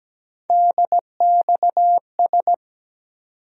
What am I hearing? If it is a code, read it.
Morse code, heard as a keyed tone, about 17 wpm, 700 Hz. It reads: DXS
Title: USA Call Sign Postfixes - 3 Letters 17wpm